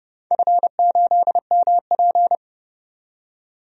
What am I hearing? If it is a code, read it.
Morse code, heard as a keyed tone, about 30 wpm, 700 Hz. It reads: F8MP
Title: DXCC-100 Call Signs - 4 Characters - 30wpm